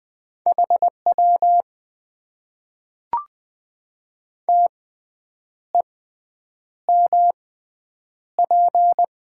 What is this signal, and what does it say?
Morse code, heard as a keyed tone, about 20 wpm, spaced as for 8 wpm, 700 Hz. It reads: HW TEMP
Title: Top 90 QSO Elements - 6x Character Spacing - 20wpm